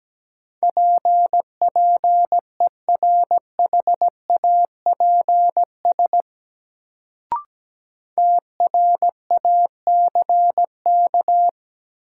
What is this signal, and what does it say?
Morse code, heard as a keyed tone, about 17 wpm, 700 Hz. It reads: PPERHAPS TRACK
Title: Sentences from Top 1100 Words - New Word Drill - 17wpm